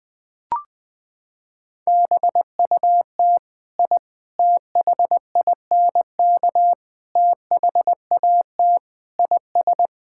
Morse code, heard as a keyed tone, about 20 wpm, 700 Hz. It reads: BUT I THINK THAT IS
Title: Sentences with 4-Word Phrases - 20wpm